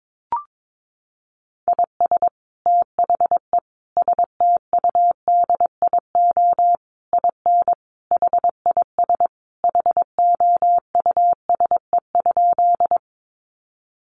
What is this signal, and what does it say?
Morse code, heard as a keyed tone, about 22 wpm, 700 Hz. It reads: IS THE STUDIO IN HIS HOUSE?
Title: Sentences from Top 1200 Words 22wpm